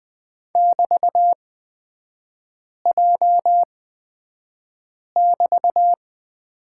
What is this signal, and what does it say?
Morse code, heard as a keyed tone, about 20 wpm, spaced as for 5 wpm, 700 Hz. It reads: <BT>J<BT>
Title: 224 Lesson 36e - BT - Three Characters 20wpm